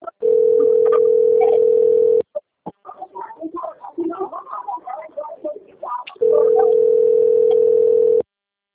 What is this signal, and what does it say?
Ringback tone (North America), 440 + 480 Hz.
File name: rg-1600-unknown- - - .93013